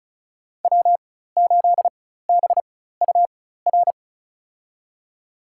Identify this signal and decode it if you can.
Morse code, heard as a keyed tone, about 34 wpm, spaced as for 16 wpm, 700 Hz. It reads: W8BUR
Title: USA Call Signs - 5 Characters - 4x Character Spacing - 35wpm